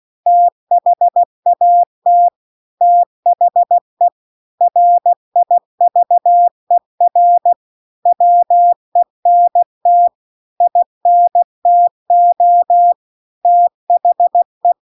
Morse code, heard as a keyed tone, about 16 wpm, 700 Hz. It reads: THAT THE RIVER WENT INTO THE